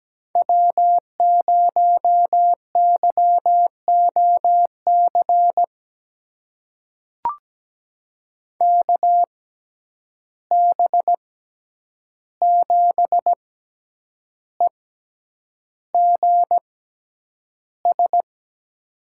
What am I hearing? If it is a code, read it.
Morse code, heard as a keyed tone, about 17 wpm, spaced as for 8 wpm, 700 Hz. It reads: W0YOC KB7EGS
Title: USA Call Signs - 6x Character Spacing - 17wpm